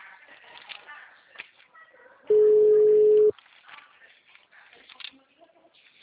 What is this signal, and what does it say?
Ringback tone (Europe, 425 Hz), with speech.
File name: out- -2023- - - .72942